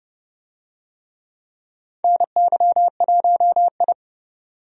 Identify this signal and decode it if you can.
Morse code, heard as a keyed tone, about 32 wpm, 700 Hz. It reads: NY1I
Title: USA Call Signs - 4 Characters - 30wpm